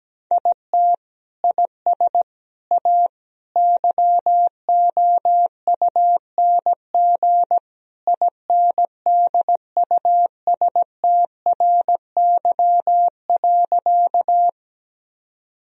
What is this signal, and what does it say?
Morse code, heard as a keyed tone, about 17 wpm, 700 Hz. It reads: IT IS A YOUNG INDUSTRY.
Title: Sentences from Top 1000 Words - Review Part 4 - 17wpm